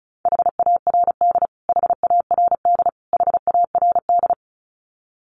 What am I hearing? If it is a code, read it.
Morse code, heard as a keyed tone, about 34 wpm, 700 Hz. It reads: HARD HARD HARD
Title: Top 500 Words Repeated 35wpm